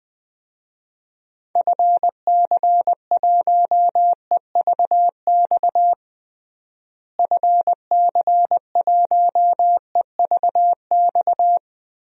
Morse code, heard as a keyed tone, about 20 wpm, 700 Hz. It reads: FC1EVX FC1EVX
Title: DXCC-100 Call Signs - 6 Characters - Repeated 3x - 20wpm